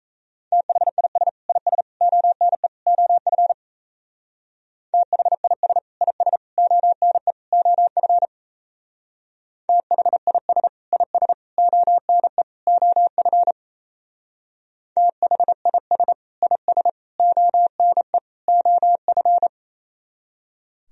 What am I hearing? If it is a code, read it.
Morse code, heard as a keyed tone, about 34 wpm, 700 Hz. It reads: THIS IS ONE OF THIS IS ONE OF THIS IS ONE OF THIS IS ONE OF
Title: Speed-Racing - Sets of 4 Words 28wpm